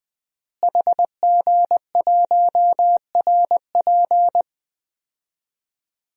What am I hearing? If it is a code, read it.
Morse code, heard as a keyed tone, about 20 wpm, 700 Hz. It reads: HG1RP
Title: DXCC-100 Call Signs - 5 Characters - Repeated 3x - 20wpm